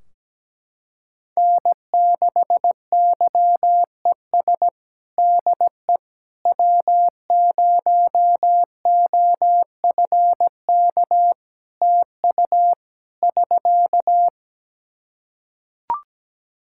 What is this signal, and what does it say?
Morse code, heard as a keyed tone, about 17 wpm, 700 Hz. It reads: N6YES DE W0OFK TU <SK>
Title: K1USN SST QSO Practice 17wpm